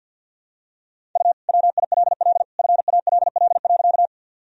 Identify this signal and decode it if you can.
Morse code, heard as a keyed tone, about 46 wpm, 700 Hz. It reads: U WILL FALL.
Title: Sentences from Top 500 Words and Abbreviations 50wpm